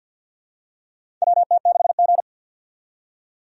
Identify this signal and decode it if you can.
Morse code, heard as a keyed tone, about 49 wpm, 700 Hz. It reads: WT6G
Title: USA Call Signs - 4 Characters - 50wpm